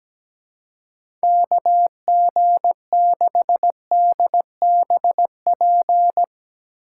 Morse code, heard as a keyed tone, about 17 wpm, 700 Hz. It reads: KG6DBP
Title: USA Call Signs - 6 Characters - 4x Character Spacing - 17wpm